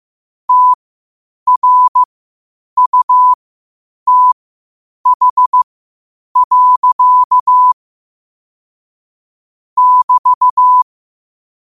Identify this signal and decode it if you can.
Morse code, heard as a keyed tone, about 15 wpm, spaced as for 8 wpm, 1000 Hz. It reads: TRUTH. <BT>
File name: qotdfiles / 8wpm